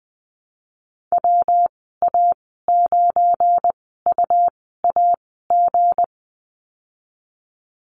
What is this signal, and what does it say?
Morse code, heard as a keyed tone, about 20 wpm, spaced as for 15 wpm, 700 Hz. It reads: WA9UAG
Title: 193 Lesson 31g - Letter v - Call Signs 20wpm